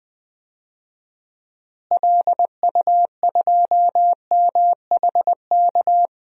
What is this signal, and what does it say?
Morse code, heard as a keyed tone, about 20 wpm, 700 Hz. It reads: LU2MHK
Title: DXCC-100 Call Signs - 6 Characters - Repeated 3x - 20wpm